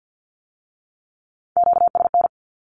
Morse code, heard as a keyed tone, about 50 wpm, 700 Hz. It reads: QSN
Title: Q-Codes with Spoken Meaning - 50wpm